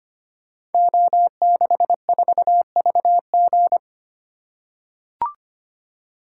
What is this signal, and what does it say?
Morse code, heard as a keyed tone, about 25 wpm, 700 Hz. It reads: O64VG
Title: Code Groups - 5 Letter:Numbers - 25wpm